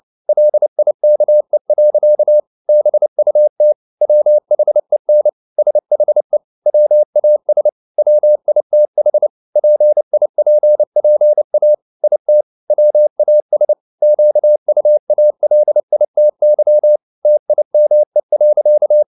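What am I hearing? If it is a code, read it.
Morse code, heard as a keyed tone, about 29 wpm, 600 Hz. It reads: LIKE. BUT WHEN SHE WAS WITH PIPPA IT WAS QUALITY TIME.
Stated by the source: Morse code at 29 wpm, sending Ken Follett's novel Never in English, from CW med Gnister.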